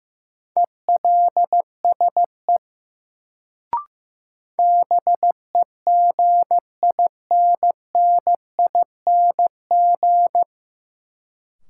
Morse code, heard as a keyed tone, about 15 wpm, 700 Hz. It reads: ELSE BEGINNING
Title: Sentences from Top 800 Words - New Word Drill - 15wpm